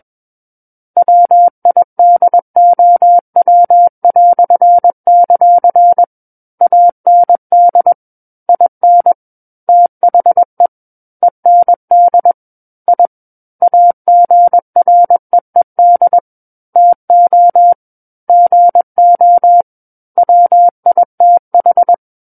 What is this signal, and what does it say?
Morse code, heard as a keyed tone, about 21 wpm, 700 Hz. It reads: WIDOW"; AND IN THE END I AGREED TO GO WITH
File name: Book One / 12-Chapter 12 - 21 WPM